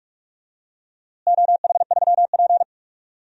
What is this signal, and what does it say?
Morse code, heard as a keyed tone, about 45 wpm, 700 Hz. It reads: OH3P
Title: DXCC-100 Call Signs - 4 Characters - Repeated 2x - 45wpm